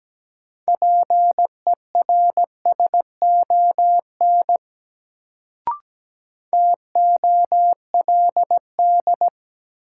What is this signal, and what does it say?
Morse code, heard as a keyed tone, about 17 wpm, 700 Hz. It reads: PERSON TOLD
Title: Sentences from Top 400 Words - New Word Drill - 17wpm